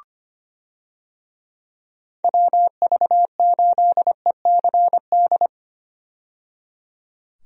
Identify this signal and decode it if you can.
Morse code, heard as a keyed tone, about 25 wpm, 700 Hz. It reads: WV8ECD
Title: USA Call Signs - 6 Characters - 25wpm